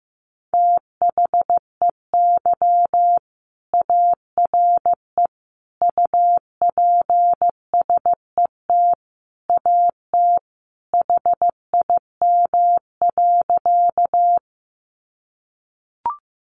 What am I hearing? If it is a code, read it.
Morse code, heard as a keyed tone, about 15 wpm, 700 Hz. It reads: THEY ARE UPSET AT HIM.
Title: Sentences from Top 800 Words 15wpm